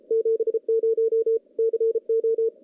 Morse code, heard as a keyed tone, about 33 wpm, 410 Hz. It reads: 70 CO
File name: contests / CW-SS-k5tr- -0400